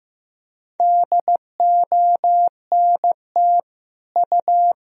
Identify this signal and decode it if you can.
Morse code, heard as a keyed tone, about 16 wpm, 700 Hz. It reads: DONT U
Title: Sentences from Top 300 Words and Abbreviations 15wpm